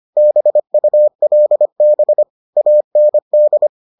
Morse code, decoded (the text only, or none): BULB AND